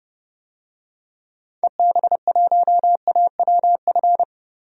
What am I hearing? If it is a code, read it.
Morse code, heard as a keyed tone, about 32 wpm, 700 Hz. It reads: EB1AWF
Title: DXCC-100 Call Signs - 6 Characters - 30wpm